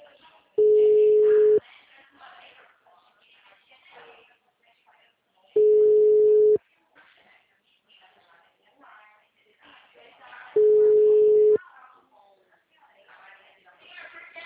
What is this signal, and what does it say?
Ringback tone (Europe, 425 Hz), with speech.